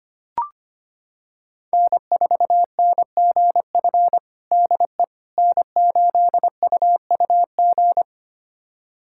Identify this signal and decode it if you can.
Morse code, heard as a keyed tone, about 25 wpm, 700 Hz. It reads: N4NGF DE N8UUG
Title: USA Callsign DE Callsign - 5 Characters - 25wpm